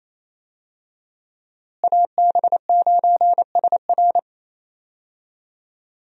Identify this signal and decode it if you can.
Morse code, heard as a keyed tone, about 28 wpm, 700 Hz. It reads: AB9SR
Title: USA Call Signs - 5 Characters - Repeated 2x - 28wpm